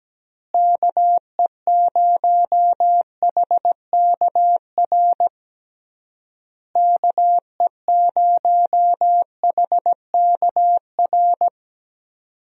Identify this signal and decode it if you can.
Morse code, heard as a keyed tone, about 17 wpm, 700 Hz. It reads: KE0HKR KE0HKR
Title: USA Call Signs - 6 Characters - Repeated 3x - 17wpm